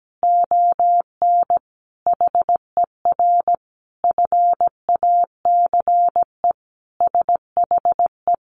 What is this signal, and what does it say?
Morse code, heard as a keyed tone, about 17 wpm, 700 Hz. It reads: ON HER FACE SHE
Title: Sentences from Top 1200 Words 17wpm